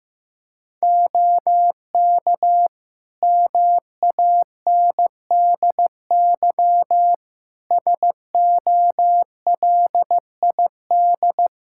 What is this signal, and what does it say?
Morse code, heard as a keyed tone, about 16 wpm, 700 Hz. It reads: OK MANDY SOLID
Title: Practice QSO - Exchange 3 - More Info - 15wpm